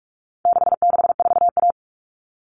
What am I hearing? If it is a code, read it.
Morse code, heard as a keyed tone, about 44 wpm, 700 Hz. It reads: 664A
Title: All Cut Numbers - 4-Digit 45wpm